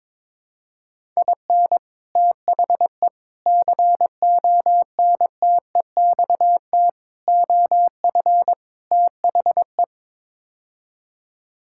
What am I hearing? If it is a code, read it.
Morse code, heard as a keyed tone, about 22 wpm, 700 Hz. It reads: IN THE CONTEXT OF THE
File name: practice / 22wpm / Build A Phrase - 5-Word Phrases - 22wpm